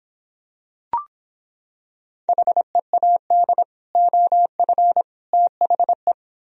Morse code, heard as a keyed tone, about 25 wpm, 700 Hz. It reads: HEAD OF THE